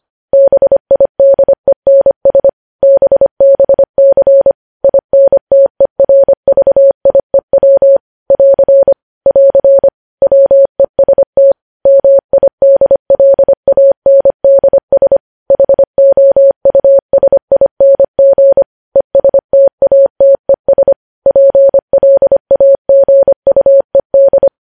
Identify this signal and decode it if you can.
Morse code, heard as a keyed tone, about 25 wpm, 550 Hz. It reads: BIDENS BBC INTERVIEW <AR> <AR> WEST MIDLANDS HOUSING ESTATES PLAGUED
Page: news headlines in morse code at 25 wpm